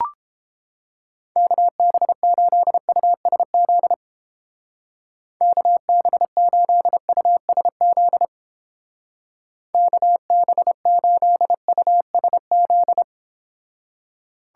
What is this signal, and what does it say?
Morse code, heard as a keyed tone, about 29 wpm, 700 Hz. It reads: KB8USZ KB8USZ KB8USZ